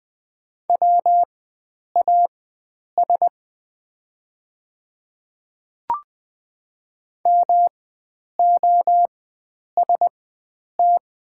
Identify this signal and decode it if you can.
Morse code, heard as a keyed tone, about 20 wpm, spaced as for 9 wpm, 690 Hz. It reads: WAS MOST
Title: 4x Character Spacing - Top 100 Words - 20wpm